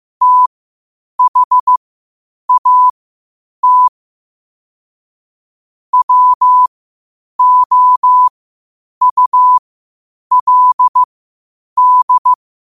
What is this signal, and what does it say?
Morse code, heard as a keyed tone, about 15 wpm, spaced as for 8 wpm, 1000 Hz. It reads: THAT WOULD
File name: qotdfiles / 8wpm